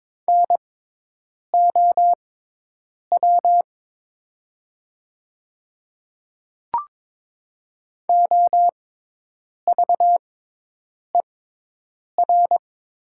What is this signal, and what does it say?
Morse code, heard as a keyed tone, about 22 wpm, spaced as for 8 wpm, 700 Hz. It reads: NOW OVER
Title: 6x Character Spacing - Top 100 Words - 22wpm